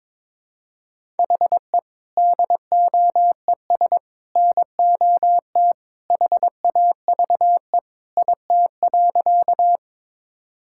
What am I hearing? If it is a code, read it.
Morse code, heard as a keyed tone, about 22 wpm, 700 Hz. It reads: HE DOES NOT HAVE IT.
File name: practice / 22wpm / Sentences with 3-Word Phrases - 22wpm